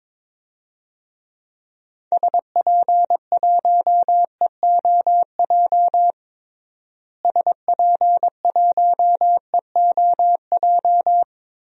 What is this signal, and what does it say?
Morse code, heard as a keyed tone, about 22 wpm, 700 Hz. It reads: SP1EOJ SP1EOJ
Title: DXCC-100 Call Signs - 6 Characters - Repeated 3x - 22wpm